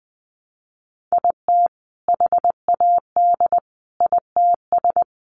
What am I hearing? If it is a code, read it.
Morse code, heard as a keyed tone, about 20 wpm, 700 Hz. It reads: IT HAD ITS
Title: Sentences from Top 1000 Words 20wpm